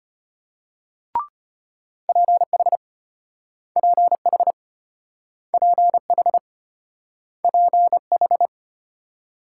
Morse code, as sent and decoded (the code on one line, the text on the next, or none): .--. .... / .--. .... / .--. .... / .--. ....
PH PH PH PH